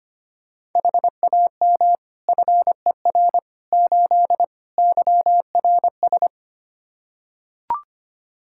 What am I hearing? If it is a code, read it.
Morse code, heard as a keyed tone, about 25 wpm, 700 Hz. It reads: HAM FER 8 YRS
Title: Practice QSO - Exchange 3 - More Info - 25wpm